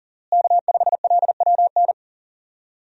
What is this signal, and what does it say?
Morse code, heard as a keyed tone, about 40 wpm, 700 Hz. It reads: K5LWN
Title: USA Call Signs - 5 Characters - Repeated 3x - 40wpm